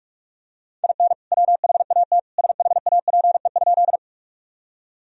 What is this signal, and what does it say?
Morse code, heard as a keyed tone, about 44 wpm, 700 Hz. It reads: IN WHAT SHAPE?
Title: Sentences from Top 1000 Words - Review Part 3 - 45wpm